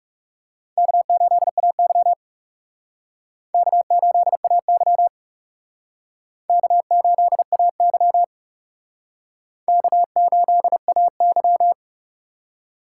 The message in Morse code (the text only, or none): K8AY K8AY K8AY K8AY